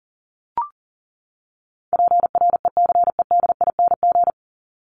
Morse code, heard as a keyed tone, about 40 wpm, 700 Hz. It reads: PRECEDING
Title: ING Suffix - Top 100 Words with Base Words 40wpm